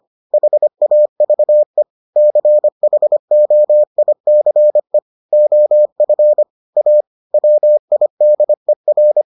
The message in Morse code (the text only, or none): HAVE CHOICE OF A WIDER